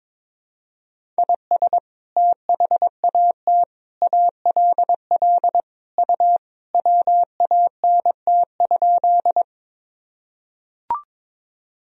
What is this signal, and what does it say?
Morse code, heard as a keyed tone, about 22 wpm, 700 Hz. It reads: IS THAT ALL U WANT?
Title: Sentences from Top 200 Words and Abbreviations 22wpm